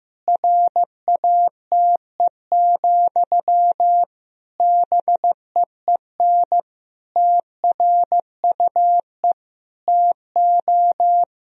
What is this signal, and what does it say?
Morse code, heard as a keyed tone, about 15 wpm, 700 Hz. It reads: RATE, BEEN TRUE TO
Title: Sentences from Top 1000 Words - Review Part 4 - 15wpm